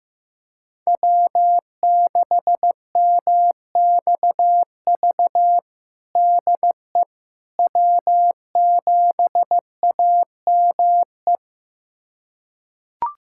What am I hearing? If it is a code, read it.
Morse code, heard as a keyed tone, about 15 wpm, 700 Hz. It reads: W6MXV DE W7AME
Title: USA Callsign DE Callsign - 5 Characters - 15wpm